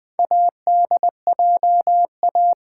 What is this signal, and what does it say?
Morse code, heard as a keyed tone, about 20 wpm, 700 Hz. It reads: ADJA
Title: Build A Word - 4 Letters At A Time - 8-Letter Words - 20wpm